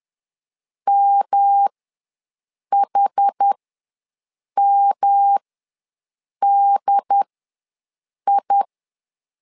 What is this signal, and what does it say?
Morse code, heard as a keyed tone, about 10 wpm, spaced as for 6 wpm, 800 Hz. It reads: MHMDI
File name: LEZIONI / 3 MWRO / 3 M ws 10wpm 2121